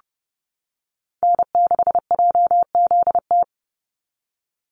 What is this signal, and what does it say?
Morse code, heard as a keyed tone, about 30 wpm, 700 Hz. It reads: N6JZT
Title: NA SOTA Activator Callsigns 30wpm